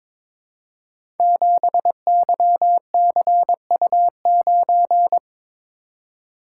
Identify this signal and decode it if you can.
Morse code, heard as a keyed tone, about 23 wpm, 700 Hz. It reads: 7YCU9
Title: Code Groups - 5 Letter:Numbers - 22wpm